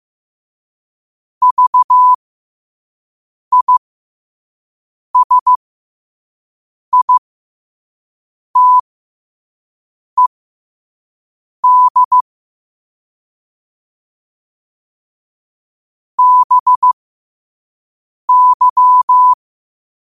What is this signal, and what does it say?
Morse code, heard as a keyed tone, about 15 wpm, spaced as for 5 wpm, 1000 Hz. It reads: VISITED BY
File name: qotdfiles / 5wpm